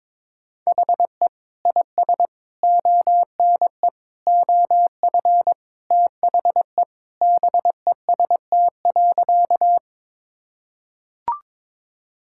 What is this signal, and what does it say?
Morse code, heard as a keyed tone, about 22 wpm, 700 Hz. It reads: HE IS ONE OF THE BEST.
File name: practice / 22wpm / Sentences with 4-Word Phrases - 22wpm